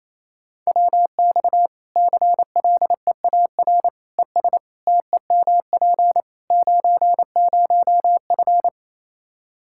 Morse code, heard as a keyed tone, about 28 wpm, 700 Hz. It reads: WX CLEAR ES TEMP 90F
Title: Practice QSO - Exchange 2 - Info - 28wpm